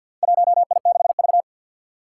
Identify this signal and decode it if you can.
Morse code, heard as a keyed tone, about 52 wpm, 700 Hz. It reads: 1I6V
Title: Four Challenging Letter-Number Combinations 50wpm